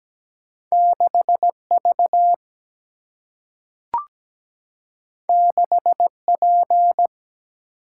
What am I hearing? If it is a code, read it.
Morse code, heard as a keyed tone, about 17 wpm, 700 Hz. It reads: 6 V 6 P
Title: Single-Letter Single-Number 17wpm